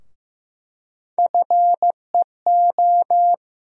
Morse code, heard as a keyed tone, about 15 wpm, 700 Hz. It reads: FEO
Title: USA Call Sign Postfixes - 3 Letters 15wpm